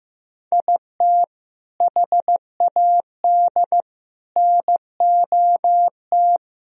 Morse code, heard as a keyed tone, about 15 wpm, 700 Hz. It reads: IT HAD NOT
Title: Sentences from Top 1000 Words - Review Part 1 - 15wpm